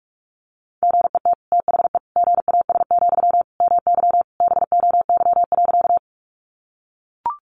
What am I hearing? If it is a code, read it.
Morse code, heard as a keyed tone, about 44 wpm, 700 Hz. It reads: GET THE GAS, MY BOY.
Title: Sentences from Top 1100 Words 45wpm